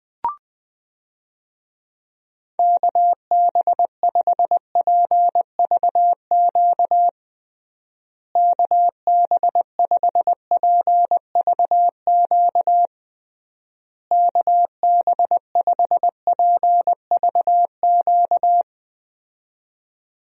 Morse code, heard as a keyed tone, about 21 wpm, 700 Hz. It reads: KB5PVQ KB5PVQ KB5PVQ
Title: USA Call Signs - 6 Characters - Repeated 3x - 20wpm